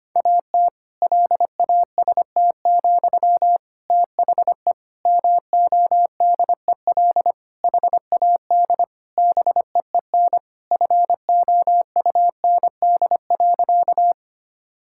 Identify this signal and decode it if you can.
Morse code, heard as a keyed tone, about 25 wpm, 700 Hz. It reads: AT LAST, THE MODEL HAD BEEN FOUND.
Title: Sentences from Top 1000 Words - Review Part 5 - 25wpm